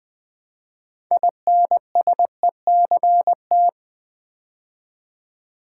Morse code, heard as a keyed tone, about 20 wpm, 700 Hz. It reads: INSECT
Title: Build A Word - 3 Letters At A Time - 6-Letter Words - 20wpm